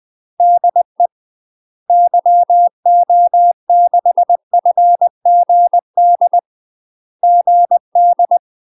Morse code, heard as a keyed tone, about 20 wpm, 700 Hz. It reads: DE YO6FGD GD
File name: UK MASTER QSOs TRAINING / 20WPM / UK MASTER QSOs 0047